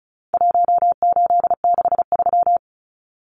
Morse code, heard as a keyed tone, about 34 wpm, 700 Hz. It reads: 1863
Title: Numbers - Contest Style - 4-Digit 35wpm